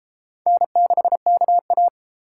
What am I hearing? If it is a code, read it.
Morse code, heard as a keyed tone, about 31 wpm, 700 Hz. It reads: N6KA